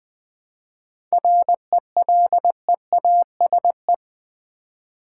Morse code, heard as a keyed tone, about 20 wpm, 700 Hz. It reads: RELEASE
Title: Sentences from Top 1100 Words - New Word Drill Alt v2 - 20wpm